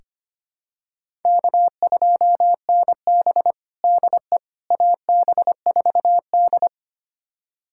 Morse code, heard as a keyed tone, about 25 wpm, 700 Hz. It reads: K2NB DE AB4D
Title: USA Callsign DE Callsign - 4 Characters - 25wpm